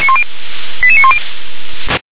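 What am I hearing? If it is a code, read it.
Five-tone selective call (ZVEI), heard as quick runs of five tones, tones of about 70 ms. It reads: 88910
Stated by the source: ZVEI five-tone selective call received on an SDR.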